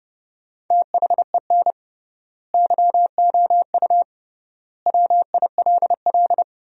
Morse code, heard as a keyed tone, about 29 wpm, 700 Hz. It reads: THEN YOU WILL
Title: 3x Word-Spacing - Sentences from Top 400 Words - 30wpm